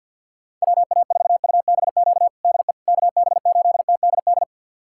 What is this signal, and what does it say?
Morse code, heard as a keyed tone, about 50 wpm, 700 Hz. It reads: WA4UBQ DE KB8TDD
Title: USA Callsign DE Callsign - 6 Characters - 50wpm